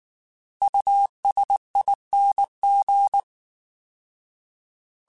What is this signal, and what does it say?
Morse code, heard as a keyed tone, about 19 wpm, 800 Hz. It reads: USING